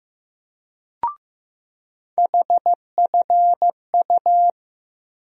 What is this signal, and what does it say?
Morse code, heard as a keyed tone, about 15 wpm, 700 Hz. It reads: HFU
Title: USA Call Sign Postfixes - 3 Letters 15wpm